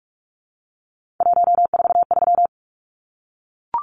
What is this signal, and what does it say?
Morse code, heard as a keyed tone, about 44 wpm, 700 Hz. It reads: 143
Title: Numbers - 3-Digit 45wpm